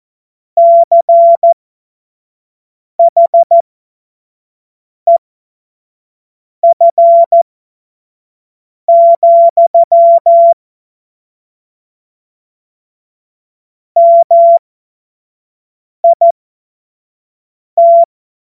Morse code, heard as a keyed tone, about 14 wpm, spaced as for 5 wpm, 680 Hz. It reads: CHEF, MIT
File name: Morsekurs / Witze / Witz 13. Juni 2023 25 F70